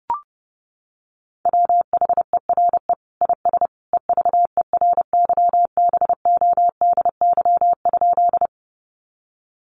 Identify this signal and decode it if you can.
Morse code, heard as a keyed tone, about 30 wpm, 700 Hz. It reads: WHERE IS EVERYBODY?
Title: Sentences from Top 1400 Words 30wpm